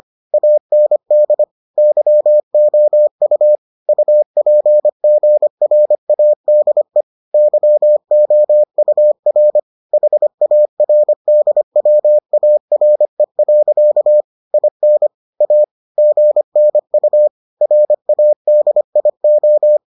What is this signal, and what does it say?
Morse code, heard as a keyed tone, about 25 wpm, 600 Hz. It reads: AND YOU UPGRADE YOUR HARDWARE. IN A GNU RADIO